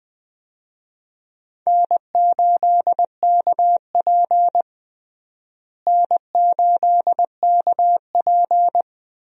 Morse code, heard as a keyed tone, about 21 wpm, 700 Hz. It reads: N8KP N8KP
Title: USA Call Signs - 4 Characters - Repeated 2x - 20wpm